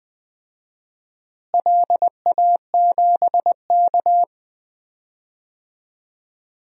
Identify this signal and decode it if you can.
Morse code, heard as a keyed tone, about 21 wpm, 700 Hz. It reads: LA7K
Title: DXCC-100 Call Signs - 4 Characters - Repeated 2x - 20wpm